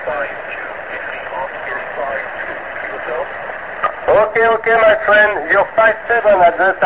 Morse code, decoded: none